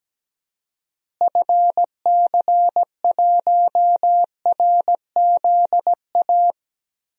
Morse code, heard as a keyed tone, about 17 wpm, 700 Hz. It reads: FC1RZA
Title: DXCC-100 Call Signs - 6 Characters - Repeated 2x - 17wpm